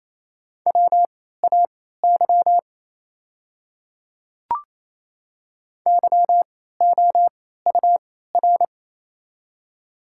Morse code, heard as a keyed tone, about 28 wpm, spaced as for 16 wpm, 700 Hz. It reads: WAY YOUR
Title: 3x Character Spacing - Top 100 Words - 28wpm